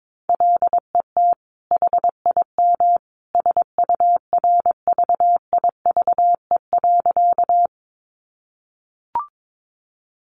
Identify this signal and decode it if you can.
Morse code, heard as a keyed tone, about 22 wpm, 700 Hz. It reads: LET HIM SURVIVE.
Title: Sentences from Top 1300 Words 22wpm